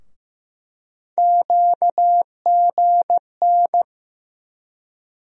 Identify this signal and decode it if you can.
Morse code, heard as a keyed tone, about 15 wpm, 700 Hz. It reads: QGN
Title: USA Call Sign Postfixes - 3 Letters 15wpm